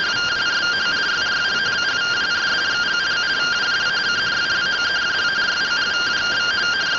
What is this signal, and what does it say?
Two-tone teleprinter signal, about 170 Hz shift.
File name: findings / CT AXIAL -256.gen-02 rtty rx